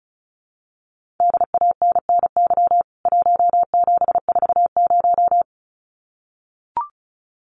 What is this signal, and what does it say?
Morse code, heard as a keyed tone, about 34 wpm, 700 Hz. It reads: DANNY 1740
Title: CWT Exchange 35wpm